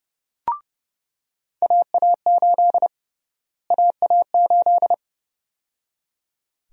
Morse code, heard as a keyed tone, about 30 wpm, 700 Hz. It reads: AA8 AA8
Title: USA Call Sign Prefixes - Repeated 2x - 30wpm